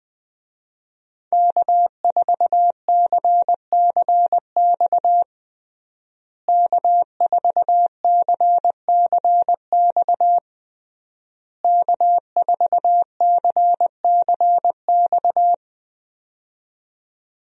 Morse code, heard as a keyed tone, about 20 wpm, 700 Hz. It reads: K4CCX K4CCX K4CCX
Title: USA Call Signs - 5 Characters - Repeated 3x - 20wpm